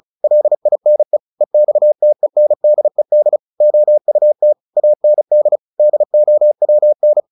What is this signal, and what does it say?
Morse code, heard as a keyed tone, about 34 wpm, 600 Hz. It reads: LINE EXTENDED OUT AND DOWN
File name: the old man and the sea / the old man and the sea 1 35wpm